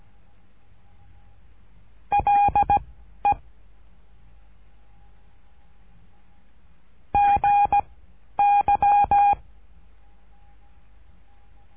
Morse code, heard as a keyed tone, about 17 wpm, 810 Hz. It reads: L E G Y
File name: cursos / CURSO DE TELEGRAFIA / Telegrafia - Trilha 13